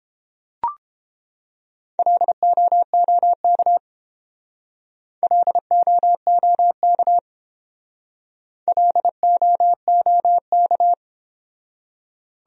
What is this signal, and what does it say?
Morse code, heard as a keyed tone, about 28 wpm, 700 Hz. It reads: LOOK LOOK LOOK